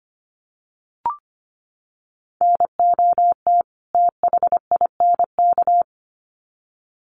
Morse code, heard as a keyed tone, about 25 wpm, 700 Hz. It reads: NOT THINK
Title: Sets of 2 Words 25wpm